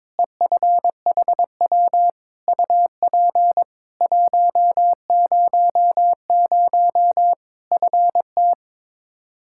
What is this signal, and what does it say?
Morse code, heard as a keyed tone, about 22 wpm, 700 Hz. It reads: EFHW UP 100 FT
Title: Practice QSO - Exchange 2 - Info - 22wpm